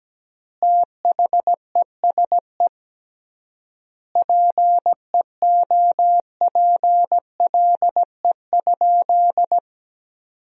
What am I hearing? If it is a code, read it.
Morse code, heard as a keyed tone, about 17 wpm, 700 Hz. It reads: THESE PEOPLE?
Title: 3x Word-Spacing - Sentences from Top 200 Words - 17wpm